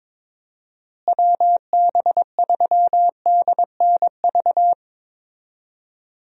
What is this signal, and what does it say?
Morse code, heard as a keyed tone, about 22 wpm, 700 Hz. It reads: WB3DNV
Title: USA Call Signs - 6 Characters - Repeated 2x - 22wpm